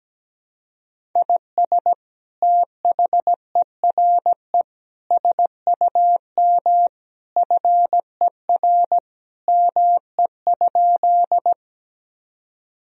Morse code, heard as a keyed tone, about 17 wpm, 700 Hz. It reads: IS THERE SUM FER ME?
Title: Sentences from Top 200 Words and Abbreviations 17wpm